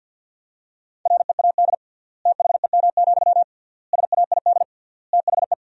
Morse code, heard as a keyed tone, about 50 wpm, 700 Hz. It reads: READ THEM, SAID THE
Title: 3x Word-Spacing - Sentences from Top 400 Words - 50wpm